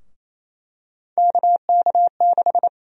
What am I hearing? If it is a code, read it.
Morse code, heard as a keyed tone, about 28 wpm, 700 Hz. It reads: KK6
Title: USA Call Sign Prefixes - Repeated 2x - 28wpm